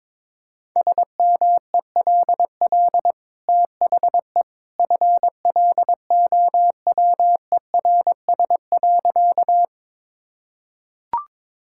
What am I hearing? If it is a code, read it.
Morse code, heard as a keyed tone, about 22 wpm, 700 Hz. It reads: SMELL THE FLOWERS.
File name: practice / 22wpm / Easy Sentences with Hints - 22wpm